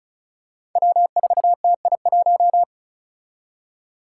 Morse code, heard as a keyed tone, about 34 wpm, 700 Hz. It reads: W4TI1
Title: Five Challenging Letter-Number Combinations 35wpm